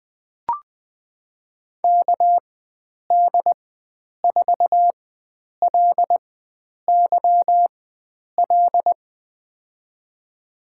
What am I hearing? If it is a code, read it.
Morse code, heard as a keyed tone, about 20 wpm, spaced as for 9 wpm, 700 Hz. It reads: KD4LYL
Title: USA Call Signs - 6 Characters - 4x Character Spacing - 20wpm